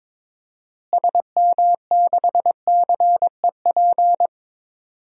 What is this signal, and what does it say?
Morse code, heard as a keyed tone, about 22 wpm, 700 Hz. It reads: SM6CEP
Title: DXCC-100 Call Signs - 6 Characters - Repeated 3x - 22wpm